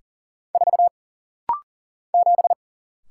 Morse code, heard as a keyed tone, about 39 wpm, 700 Hz. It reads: 4 7
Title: Single Letter-Number - Rapid-Fire 40wpm